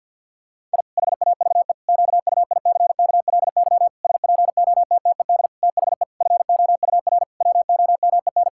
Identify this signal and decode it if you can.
Morse code, heard as a keyed tone, about 52 wpm, 700 Hz. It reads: I HAVE QUICKLY SPOTTED THE FOUR WOMEN
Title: Pangram Challenge v2 - 50wpm